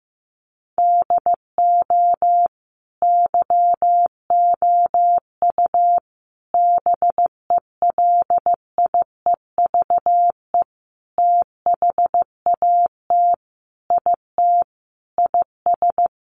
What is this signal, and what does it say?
Morse code, heard as a keyed tone, about 15 wpm, 700 Hz. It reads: DO YOU BELIEVE THAT IT IS
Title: Sentences from Top 1500 Words 15wpm